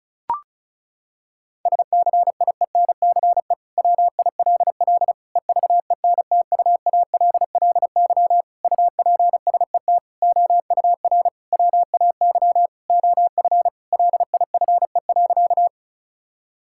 Morse code, heard as a keyed tone, about 34 wpm, 700 Hz. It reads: SCIENCE WILL EVENTUALLY UPSET OUR WAY OF LIFE.
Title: Sentences from Top 1000 Words - Review Part 6 - 35wpm